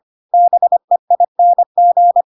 Morse code, heard as a keyed tone, about 25 wpm, 700 Hz. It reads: BEING
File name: Pinocchio / Pinocchio ch4 25 25 ik1hge